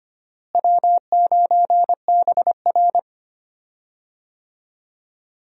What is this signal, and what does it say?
Morse code, heard as a keyed tone, about 25 wpm, 700 Hz. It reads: W9BR
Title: USA Call Signs - 4 Characters - Repeated 3x - 25wpm